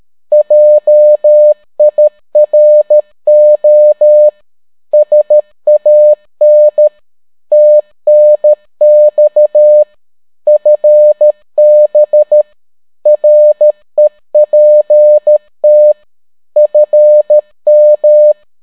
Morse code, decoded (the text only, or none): JIRO SAN TNX FB REPT FM